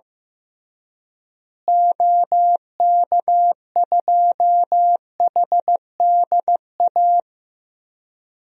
Morse code, heard as a keyed tone, about 15 wpm, 700 Hz. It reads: OK2HDA
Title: DXCC-100 Call Signs - 6 Characters - Repeated 3x - 15wpm